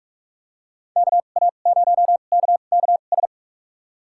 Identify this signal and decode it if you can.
Morse code, heard as a keyed tone, about 46 wpm, spaced as for 32 wpm, 700 Hz. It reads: KA0KKS
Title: USA Call Signs - 6 Characters - 2x Character Spacing - 45wpm